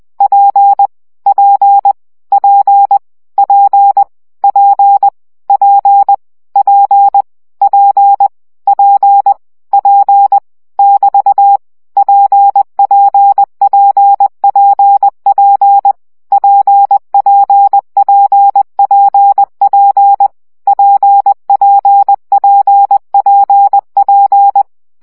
Morse code, heard as a keyed tone, about 20 wpm, 800 Hz. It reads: P P P P P P P P P P <BT> PPPPP PPPPP PPPPP
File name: CW gyakorlas p 17 baud